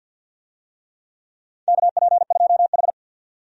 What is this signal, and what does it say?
Morse code, heard as a keyed tone, about 52 wpm, 700 Hz. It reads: KP2H
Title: DXCC-100 Call Signs - 4 Characters - 50wpm